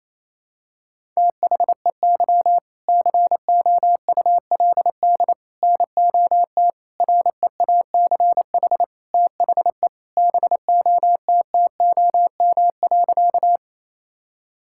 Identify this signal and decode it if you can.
Morse code, heard as a keyed tone, about 27 wpm, 700 Hz. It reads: THEY COULD NOT REACH THE BOTTOM.
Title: Sentences from Top 1000 Words - Review Part 6 - 28wpm